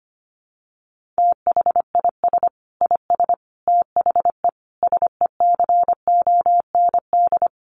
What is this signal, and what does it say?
Morse code, heard as a keyed tone, about 25 wpm, 700 Hz. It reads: THIS IS THE SECOND
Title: Sentences from Top 1400 Words 25wpm